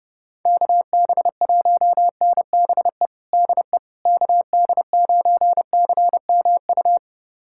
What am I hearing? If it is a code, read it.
Morse code, heard as a keyed tone, about 32 wpm, 700 Hz. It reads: KB1NBE DE KD9CMU
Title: USA Callsign DE Callsign - 6 Characters - 30wpm